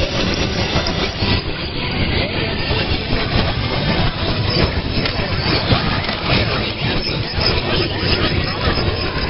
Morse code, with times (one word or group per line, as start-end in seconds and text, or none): none